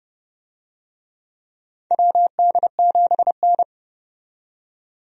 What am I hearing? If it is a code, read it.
Morse code, heard as a keyed tone, about 32 wpm, 700 Hz. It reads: WD7N
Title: USA Call Signs - 4 Characters - 30wpm